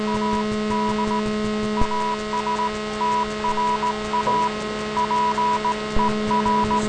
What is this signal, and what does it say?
Morse code, heard as a keyed tone, about 17 wpm, 980 Hz. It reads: AD ASTRA PER